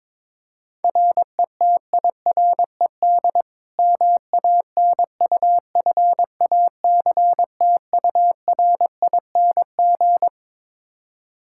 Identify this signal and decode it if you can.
Morse code, heard as a keyed tone, about 22 wpm, 700 Hz. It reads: RETIRED MANUFACTURING
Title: Practice QSO - Exchange 3 - More Info - 22wpm